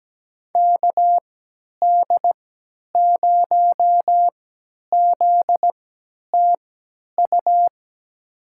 Morse code, heard as a keyed tone, about 17 wpm, spaced as for 10 wpm, 700 Hz. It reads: KD0ZTU
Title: USA Call Signs - 3x Character Spacing - 17wpm